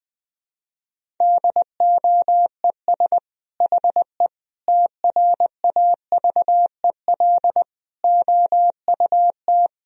Morse code, heard as a keyed tone, about 20 wpm, 700 Hz. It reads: DOES HE TRAVEL OUT
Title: Sentences from Top 1000 Words - Review Part 2 - 20wpm